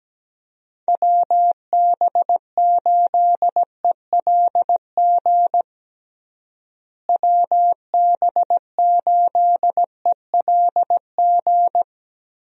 Morse code, heard as a keyed tone, about 17 wpm, 700 Hz. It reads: WB8ELG WB8ELG
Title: USA Call Signs - 6 Characters - Repeated 2x - 17wpm